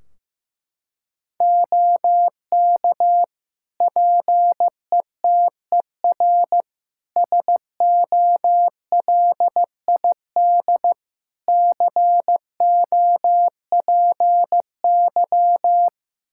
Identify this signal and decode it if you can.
Morse code, heard as a keyed tone, about 16 wpm, 700 Hz. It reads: OK PETER SOLID COPY
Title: K1USN SST QSO Practice 15wpm